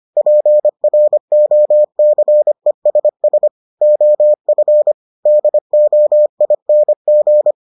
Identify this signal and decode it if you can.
Morse code, heard as a keyed tone, about 25 wpm, 600 Hz. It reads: PROCESS OF DOING